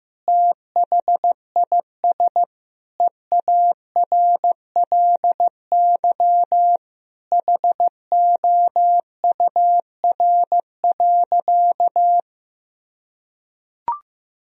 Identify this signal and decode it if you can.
Morse code, heard as a keyed tone, about 16 wpm, 700 Hz. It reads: THIS EARLY HOUR.
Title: Sentences from Top 1000 Words - Review Part 2 - 15wpm